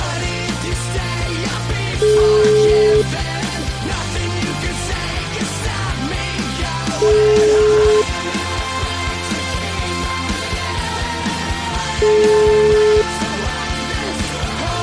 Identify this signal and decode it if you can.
Ringback tone (Europe, 425 Hz), with noise.